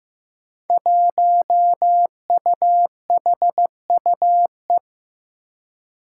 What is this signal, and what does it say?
Morse code, heard as a keyed tone, about 15 wpm, 700 Hz. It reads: 1UHUE
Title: Five Challenging Letter-Number Combinations 15wpm